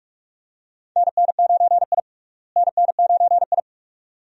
Morse code, heard as a keyed tone, about 46 wpm, 700 Hz. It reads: NN9I NN9I
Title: USA Call Signs - 4 Characters - Repeated 2x - 45wpm